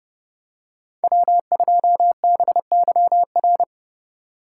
Morse code, heard as a keyed tone, about 29 wpm, 700 Hz. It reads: W2BYR